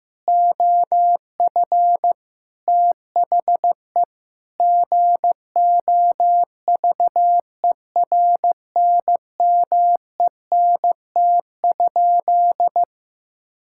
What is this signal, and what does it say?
Morse code, heard as a keyed tone, about 15 wpm, 700 Hz. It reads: OF THE GOVERNMENT?
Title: Sentences from Top 1000 Words - Review Part 5 - 15wpm